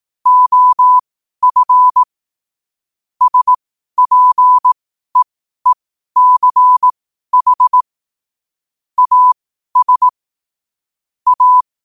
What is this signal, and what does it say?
Morse code, heard as a keyed tone, about 18 wpm, spaced as for 12 wpm, 1000 Hz. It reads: OF SPEECH AS A